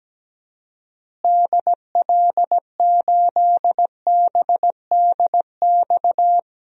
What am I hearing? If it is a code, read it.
Morse code, heard as a keyed tone, about 17 wpm, 700 Hz. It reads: DL8BDX
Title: DXCC-100 Call Signs - 6 Characters - Repeated 3x - 17wpm